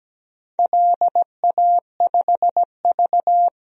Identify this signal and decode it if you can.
Morse code, heard as a keyed tone, about 17 wpm, 700 Hz. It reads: LA5V
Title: DXCC-100 Call Signs - 4 Characters - 17wpm